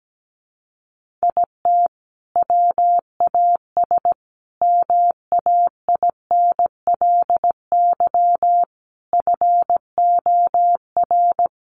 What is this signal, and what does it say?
Morse code, heard as a keyed tone, about 17 wpm, 700 Hz. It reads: IT WAS MAINLY FOR
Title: Sentences from Top 1400 Words 17wpm